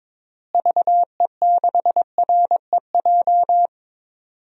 Morse code, heard as a keyed tone, about 22 wpm, 700 Hz. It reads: VE6REJ
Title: DXCC-100 Call Signs - 6 Characters - Repeated 3x - 22wpm